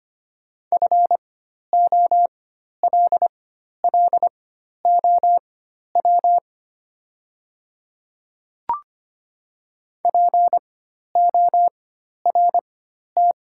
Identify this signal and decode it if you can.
Morse code, heard as a keyed tone, about 25 wpm, spaced as for 12 wpm, 700 Hz. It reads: FOLLOW PORT
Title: 4x Character Spacing - Top 200 Words - 25wpm